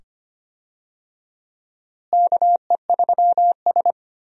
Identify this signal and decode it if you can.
Morse code, heard as a keyed tone, about 25 wpm, 700 Hz. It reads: KE3S
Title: USA Call Signs - 4 Characters - Repeated 2x - 25wpm